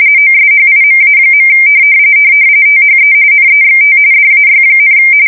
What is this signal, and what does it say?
RTTY, about 45.45 baud. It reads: OSHIMA. THE NEXT DAY,
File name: fsk170 / USS-Massachusetts